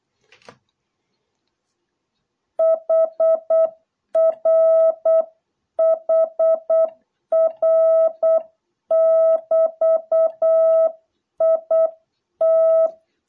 Morse code, heard as a keyed tone, about 8 wpm, 660 Hz. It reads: HR HR<BT>IT